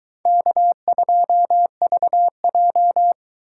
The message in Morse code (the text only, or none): K2VJ